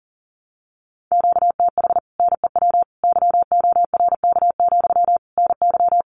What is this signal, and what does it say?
Morse code, heard as a keyed tone, about 40 wpm, 700 Hz. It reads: QTH NEW YORK, NY
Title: Common QSO Phrases Encore 40wpm